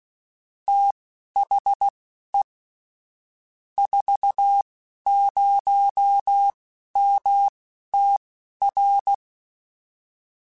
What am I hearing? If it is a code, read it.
Morse code, heard as a keyed tone, about 16 wpm, spaced as for 11 wpm, 800 Hz. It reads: THE 40MTR
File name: QSOs / QSOs-mp3s / kc7haa@10